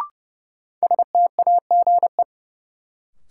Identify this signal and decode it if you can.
Morse code, heard as a keyed tone, about 30 wpm, 700 Hz. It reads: STAGE
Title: Sentences from Top 1200 Words - New Word Drill Alt - 30wpm